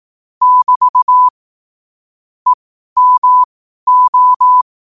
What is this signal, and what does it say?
Morse code, heard as a keyed tone, about 18 wpm, spaced as for 12 wpm, 1000 Hz. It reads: <BT> EMO
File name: qotdfiles / 12wpm